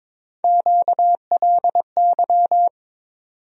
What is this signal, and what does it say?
Morse code, heard as a keyed tone, about 22 wpm, 700 Hz. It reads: QLY
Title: USA Call Sign Postfixes - 3 Letters 22wpm